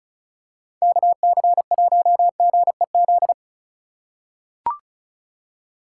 Morse code, heard as a keyed tone, about 34 wpm, 700 Hz. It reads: KC1GEZ
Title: USA Call Signs - 3x Character Spacing - 35wpm